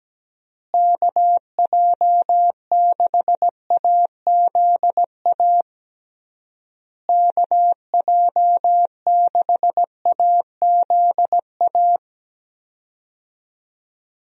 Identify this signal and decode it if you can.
Morse code, heard as a keyed tone, about 17 wpm, 700 Hz. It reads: KJ6AZA KJ6AZA
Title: USA Call Signs - 6 Characters - Repeated 3x - 17wpm